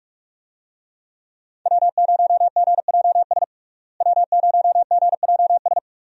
Morse code, heard as a keyed tone, about 46 wpm, 700 Hz. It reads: W0GJS W0GJS
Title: USA Call Signs - 5 Characters - Repeated 3x - 45wpm